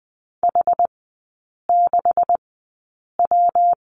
Morse code, heard as a keyed tone, about 20 wpm, spaced as for 8 wpm, 700 Hz. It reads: H6W
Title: 127 Lesson 21e - Number 6 - Three Characters 20wpm